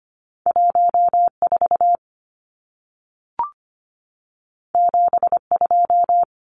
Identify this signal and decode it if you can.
Morse code, heard as a keyed tone, about 25 wpm, 700 Hz. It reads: 14 72
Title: Numbers - 2-Digit 25wpm